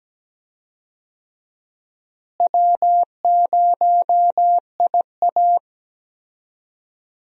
Morse code, heard as a keyed tone, about 17 wpm, 700 Hz. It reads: W0IA
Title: USA Call Signs - 4 Characters - Repeated 3x - 17wpm